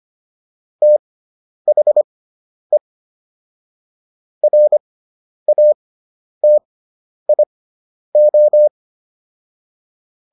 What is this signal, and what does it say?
Morse code, heard as a keyed tone, about 25 wpm, spaced as for 10 wpm, 600 Hz. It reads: THE RATIO